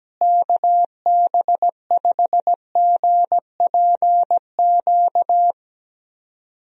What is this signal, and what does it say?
Morse code, heard as a keyed tone, about 17 wpm, 700 Hz. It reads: KB5GPQ
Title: USA Call Signs - 6 Characters - Repeated 3x - 17wpm